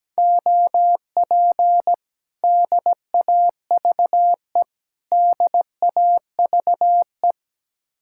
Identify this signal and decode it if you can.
Morse code, heard as a keyed tone, about 17 wpm, 700 Hz. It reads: OP DAVE DAVE
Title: Speed-Racing Common QSO Phrases - 17wpm